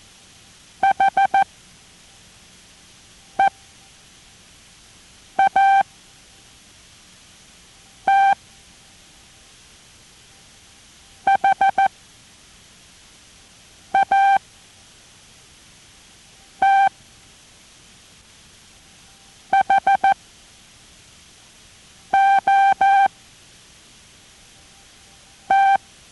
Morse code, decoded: HEAT HAT HOT